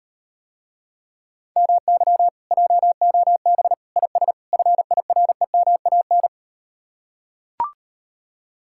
Morse code, heard as a keyed tone, about 39 wpm, 700 Hz. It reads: MY JOB IS FIREMAN